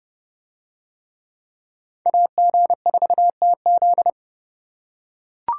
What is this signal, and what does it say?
Morse code, heard as a keyed tone, about 30 wpm, 700 Hz. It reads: AG4TZ